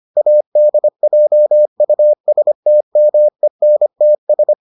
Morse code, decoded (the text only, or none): ADJUSTMENTS